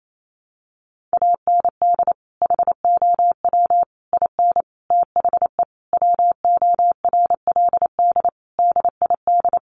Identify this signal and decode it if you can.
Morse code, heard as a keyed tone, about 28 wpm, 700 Hz. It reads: AND HOW IN THE WORLD DID
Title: Sentences from Top 200 Words 28wpm